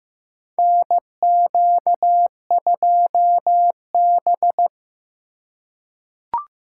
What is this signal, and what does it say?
Morse code, heard as a keyed tone, about 15 wpm, 700 Hz. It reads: NQ2B
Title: USA Call Signs - 4 Characters - 3x Character Spacing - 15wpm